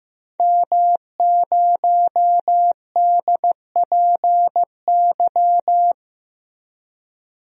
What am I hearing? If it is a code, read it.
Morse code, heard as a keyed tone, about 15 wpm, 700 Hz. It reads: M0DPY
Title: DXCC-100 Call Signs - 5 Characters - Repeated 3x - 15wpm